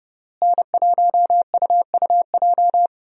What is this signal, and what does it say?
Morse code, heard as a keyed tone, about 32 wpm, 700 Hz. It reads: N1UUJ
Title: USA Call Signs - 5 Characters - Repeated 3x - 30wpm